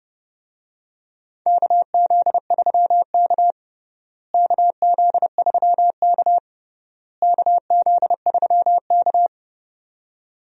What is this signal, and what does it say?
Morse code, heard as a keyed tone, about 30 wpm, 700 Hz. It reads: KZ3K KZ3K KZ3K
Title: USA Call Signs - 4 Characters - Repeated 3x - 30wpm